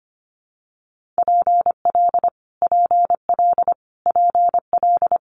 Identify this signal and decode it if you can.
Morse code, heard as a keyed tone, about 25 wpm, 700 Hz. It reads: PL PL PL
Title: Multi-letter Phonemes Repeated 3x 25wpm